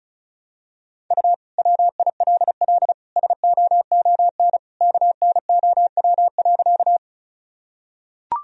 Morse code, heard as a keyed tone, about 36 wpm, 700 Hz. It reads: U WILL SOON KNOW.
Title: Sentences from Top 400 Words and Abbreviations 35wpm